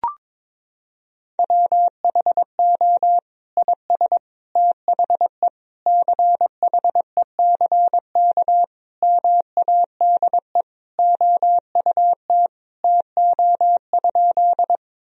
Morse code, heard as a keyed tone, about 21 wpm, 700 Hz. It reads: WHO IS THE CHECK MADE OUT TO?
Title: Sentences from Top 1000 Words - Review Part 3 - 22wpm